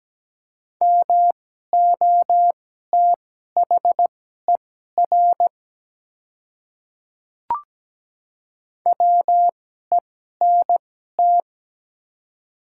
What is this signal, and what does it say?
Morse code, heard as a keyed tone, about 17 wpm, spaced as for 12 wpm, 700 Hz. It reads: MOTHER WENT
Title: 2x Character Spacing - Top 200 Words - 17wpm